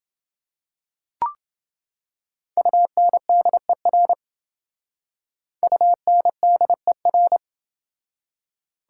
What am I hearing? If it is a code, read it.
Morse code, heard as a keyed tone, about 29 wpm, 700 Hz. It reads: UNDER UNDER